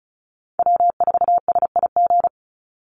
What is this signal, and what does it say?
Morse code, heard as a keyed tone, about 34 wpm, 700 Hz. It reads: W4SIG
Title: Call Signs - 35wpm